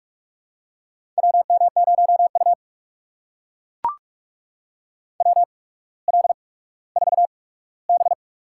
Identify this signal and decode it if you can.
Morse code, heard as a keyed tone, about 44 wpm, spaced as for 19 wpm, 700 Hz. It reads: WM0U WL4B
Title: USA Call Signs - 4 Characters - 8x Character Spacing - 45wpm